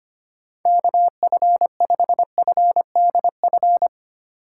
Morse code, heard as a keyed tone, about 25 wpm, 700 Hz. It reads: KF5FDF
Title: USA Call Signs - 6 Characters - 4x Character Spacing - 25wpm